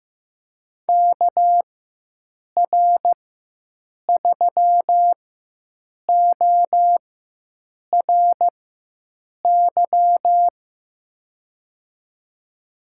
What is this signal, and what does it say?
Morse code, heard as a keyed tone, about 15 wpm, spaced as for 7 wpm, 700 Hz. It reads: KR3ORY
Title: USA Call Signs - 6 Characters - 4x Character Spacing - 15wpm